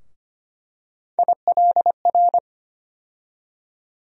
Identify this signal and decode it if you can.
Morse code, heard as a keyed tone, about 25 wpm, 700 Hz. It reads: ILR
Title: USA Call Sign Postfixes - 3 Letters 25wpm